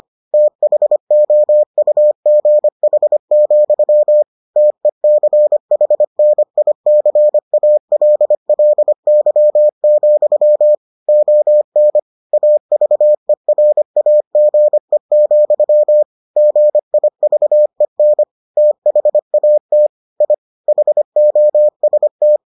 Morse code, heard as a keyed tone, about 25 wpm, 600 Hz. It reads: THOUGH, TECHNICALLY, ON AVERAGE, GIVEN THAT I HOST